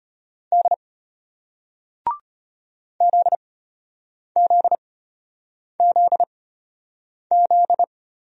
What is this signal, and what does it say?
Morse code, heard as a keyed tone, about 34 wpm, spaced as for 8 wpm, 700 Hz. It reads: D ZZZZ